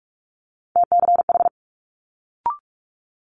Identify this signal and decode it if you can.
Morse code, heard as a keyed tone, about 46 wpm, 700 Hz. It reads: TCH
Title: Multi-letter Phonemes Quick Fire 45wpm